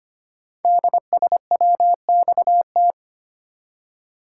Morse code, heard as a keyed tone, about 25 wpm, 700 Hz. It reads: DSWXT
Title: Code Groups - 5 Letters - 25wpm